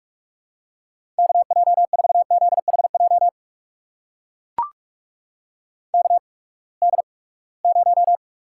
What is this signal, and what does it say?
Morse code, heard as a keyed tone, about 44 wpm, spaced as for 25 wpm, 700 Hz. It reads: KJ4ZHJ KD0
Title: USA Call Signs - 6 Characters - 8x Character Spacing - 45wpm